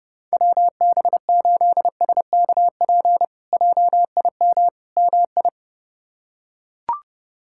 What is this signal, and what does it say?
Morse code, heard as a keyed tone, about 32 wpm, 700 Hz. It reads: WB8SKP JIM MI
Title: K1USN SST QSO Practice - 30wpm